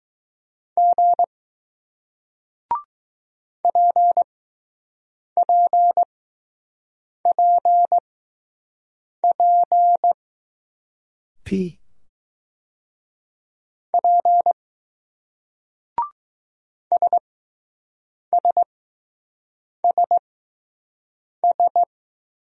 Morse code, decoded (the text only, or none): G PPPP P SSSS